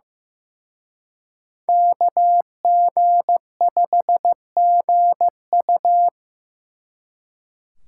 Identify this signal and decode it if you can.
Morse code, heard as a keyed tone, about 18 wpm, 700 Hz. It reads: KG5GU
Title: Speed Racing - USA Call Signs 15wpm